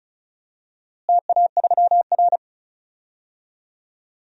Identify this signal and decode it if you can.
Morse code, heard as a keyed tone, about 36 wpm, 690 Hz. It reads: TA3R
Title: DXCC-100 Call Signs - 4 Characters - Repeated 2x - 35wpm